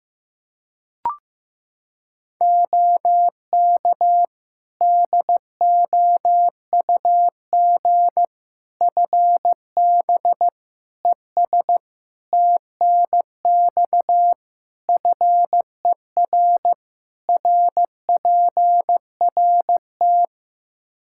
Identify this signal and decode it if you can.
Morse code, heard as a keyed tone, about 15 wpm, 700 Hz. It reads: OK DOUG FB ES TNX FER RPRT
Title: K1USN SST QSO Practice 15wpm